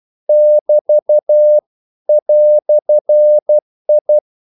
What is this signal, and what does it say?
Morse code, heard as a keyed tone, about 12 wpm, 600 Hz. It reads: <BT> "I
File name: CW / Practice / EnglishQuotations0312